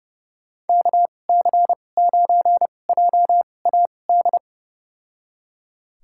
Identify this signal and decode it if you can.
Morse code, heard as a keyed tone, about 30 wpm, spaced as for 22 wpm, 700 Hz. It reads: KC9JAD
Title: USA Call Signs - 6 Characters - 2x Character Spacing - 30wpm